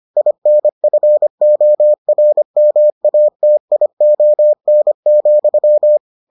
Morse code, heard as a keyed tone, about 25 wpm, 600 Hz. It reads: INFORMATION,